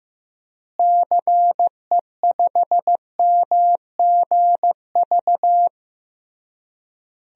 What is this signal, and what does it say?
Morse code, heard as a keyed tone, about 15 wpm, 700 Hz. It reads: CE5MGV
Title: DXCC-100 Call Signs - 6 Characters - Repeated 3x - 15wpm